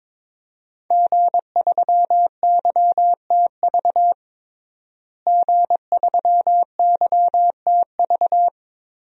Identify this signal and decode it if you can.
Morse code, heard as a keyed tone, about 22 wpm, 700 Hz. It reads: G3YTV G3YTV
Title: DXCC-100 Call Signs - 5 Characters - Repeated 3x - 22wpm